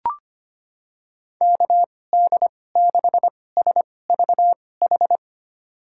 Morse code, heard as a keyed tone, about 25 wpm, spaced as for 18 wpm, 700 Hz. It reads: KD6SVH
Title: USA Call Signs - 2x Character Spacing - 25wpm